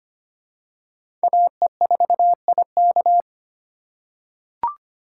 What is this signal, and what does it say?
Morse code, heard as a keyed tone, about 25 wpm, 700 Hz. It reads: AE4IK
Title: USA Call Signs - 5 Characters - 3x Character Spacing - 25wpm